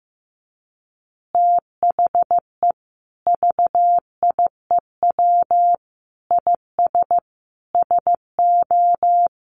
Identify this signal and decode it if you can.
Morse code, heard as a keyed tone, about 15 wpm, 700 Hz. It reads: THE VIEW IS SO
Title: Additional Easy Phrases 15wpm